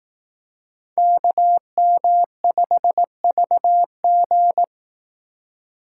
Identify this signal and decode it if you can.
Morse code, heard as a keyed tone, about 18 wpm, 700 Hz. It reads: KM5VG
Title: Speed Racing - USA 2x2 Call Signs 15wpm